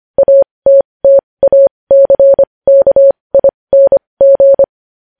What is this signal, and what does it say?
Morse code, heard as a keyed tone, about 25 wpm, spaced as for 20 wpm, 550 Hz. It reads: ATTACKING
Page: news headlines in morse code at 20 wpm